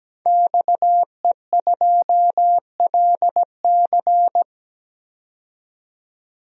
Morse code, heard as a keyed tone, about 17 wpm, 700 Hz. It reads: XE2LC
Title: DXCC-100 Call Signs - 5 Characters - 17wpm